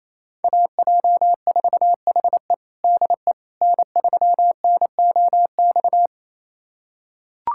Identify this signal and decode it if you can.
Morse code, heard as a keyed tone, about 28 wpm, 700 Hz. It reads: AJ4HE DE N3NOX
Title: USA Callsign DE Callsign - 5 Characters - 28wpm